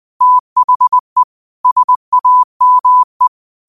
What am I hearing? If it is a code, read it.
Morse code, heard as a keyed tone, about 20 wpm, 1000 Hz. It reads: THE SAME